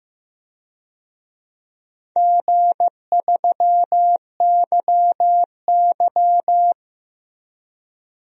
Morse code, heard as a keyed tone, about 15 wpm, 700 Hz. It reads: G3YY
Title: DXCC-100 Call Signs - 4 Characters - 15wpm